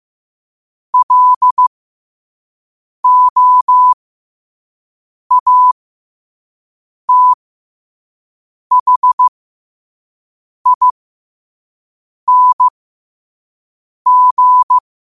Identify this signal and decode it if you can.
Morse code, heard as a keyed tone, about 15 wpm, spaced as for 5 wpm, 1000 Hz. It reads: LOATHING
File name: qotdfiles / 5wpm